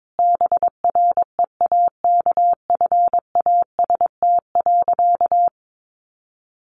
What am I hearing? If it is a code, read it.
Morse code, heard as a keyed tone, about 22 wpm, 700 Hz. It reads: BREAKFAST.
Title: Easy Phrases 22wpm